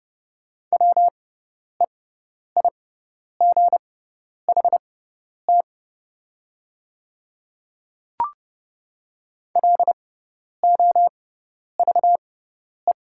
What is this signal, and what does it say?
Morse code, heard as a keyed tone, about 30 wpm, spaced as for 10 wpm, 700 Hz. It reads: WEIGHT LOVE
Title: 6x Character Spacing - Top 500 Words - 30wpm